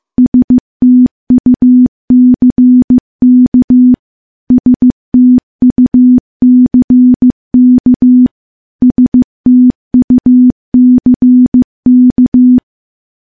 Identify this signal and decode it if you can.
Morse code, heard as a keyed tone, about 15 wpm, 250 Hz. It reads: STUCK STUCK STUCK
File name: morse / morse-training-15WPM